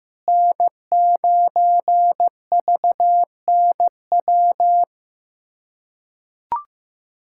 Morse code, heard as a keyed tone, about 15 wpm, 700 Hz. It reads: N9VNW
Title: USA Callsign DE Callsign - 5 Characters - 15wpm